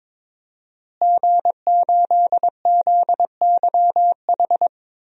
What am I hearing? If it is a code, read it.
Morse code, heard as a keyed tone, about 22 wpm, 700 Hz. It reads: G8ZYH
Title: DXCC-100 Call Signs - 5 Characters - Repeated 3x - 22wpm